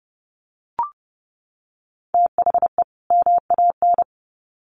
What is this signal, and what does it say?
Morse code, heard as a keyed tone, about 30 wpm, 700 Hz. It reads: THE MAN
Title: Sets of 2 Words 30wpm